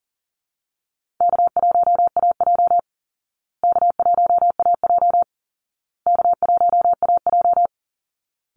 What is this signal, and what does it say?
Morse code, heard as a keyed tone, about 40 wpm, 700 Hz. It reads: K1AJ K1AJ K1AJ
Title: Easy Call Signs Repeated 3x - 40wpm